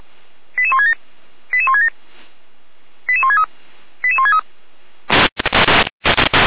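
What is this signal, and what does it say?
Five-tone selective call (ZVEI), heard as quick runs of five tones, tones of about 70 ms. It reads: 88167 88167 88163 88163
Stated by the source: ZVEI five-tone selective call 88459 received on an SDR.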